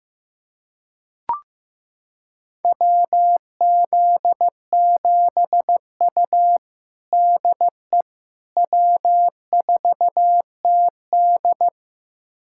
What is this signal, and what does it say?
Morse code, heard as a keyed tone, about 15 wpm, 700 Hz. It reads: WZ7U DE W4TD
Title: USA Callsign DE Callsign - 4 Characters - 15wpm